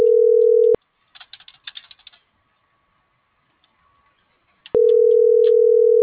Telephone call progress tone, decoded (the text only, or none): ringback tone